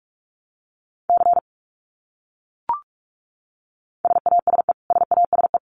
Morse code, heard as a keyed tone, about 44 wpm, 700 Hz. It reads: C SASE SASE
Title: Abbreviations Repeated 3x - 45wpm